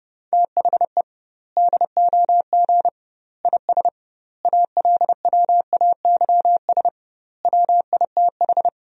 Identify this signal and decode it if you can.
Morse code, heard as a keyed tone, about 30 wpm, 700 Hz. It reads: THE DOG IS ALWAYS WITH
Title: 2x Word-Spacing - Sentences from Top 400 Words - 30wpm